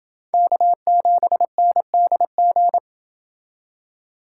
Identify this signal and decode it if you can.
Morse code, heard as a keyed tone, about 28 wpm, 700 Hz. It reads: K7NDG